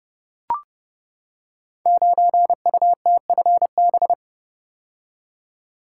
Morse code, heard as a keyed tone, about 30 wpm, 700 Hz. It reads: 9UTFB
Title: Code Groups - 5 Letter:Numbers - 30wpm